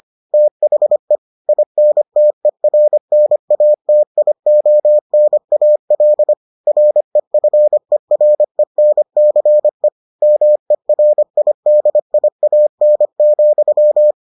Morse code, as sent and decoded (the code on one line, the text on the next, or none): - .... / .. -. - . .-. -. .- - .. --- -. .- .-.. / .-. ..-. .-. -. -.-. / -- . .-. .. -.. .. .- -. --..--
THE INTERNATIONAL REFERENCE MERIDIAN,